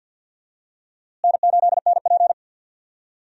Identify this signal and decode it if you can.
Morse code, heard as a keyed tone, about 46 wpm, 700 Hz. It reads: N8NP
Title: USA Call Signs - 4 Characters - Repeated 2x - 50wpm